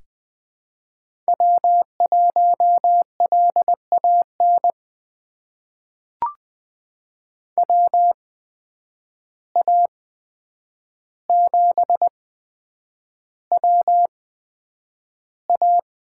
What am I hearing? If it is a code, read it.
Morse code, heard as a keyed tone, about 20 wpm, spaced as for 9 wpm, 700 Hz. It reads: W1LAN WA7WA
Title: USA Call Signs - 5 Characters - 8x Character Spacing - 20wpm